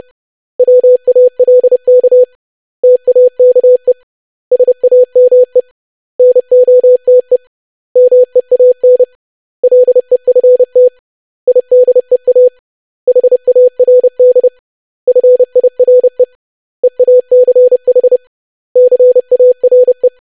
Morse code, heard as a keyed tone, about 30 wpm, 500 Hz. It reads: WALK TAKE SAME NOTE MEAN LEFT IDEA HARD FIRE EACH CARE
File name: lesson / web / basic500-25wpm